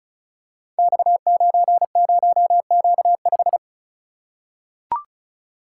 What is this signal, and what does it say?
Morse code, heard as a keyed tone, about 36 wpm, 700 Hz. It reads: X90Q5
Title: Code Groups - 5 Letter:Numbers - 35wpm